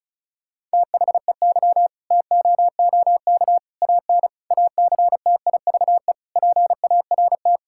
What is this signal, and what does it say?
Morse code, heard as a keyed tone, about 36 wpm, 700 Hz. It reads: THEY TOOK AN ACTIVE PART
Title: Sentences from Top 1000 Words - Review Part 6 - 35wpm